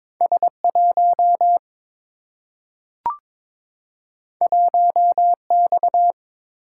Morse code, heard as a keyed tone, about 22 wpm, 700 Hz. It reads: S 1 1 X
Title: Single-Letter Single-Number 22wpm